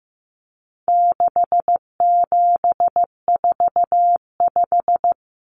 Morse code, heard as a keyed tone, about 15 wpm, 700 Hz. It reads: 6745
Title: Numbers - 4-Digit 15wpm